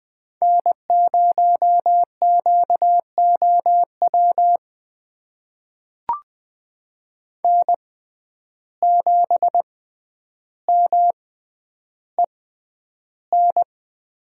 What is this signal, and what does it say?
Morse code, heard as a keyed tone, about 20 wpm, spaced as for 10 wpm, 700 Hz. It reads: N0QOW N7MEN
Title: USA Call Signs - 5 Characters - 6x Character Spacing - 20wpm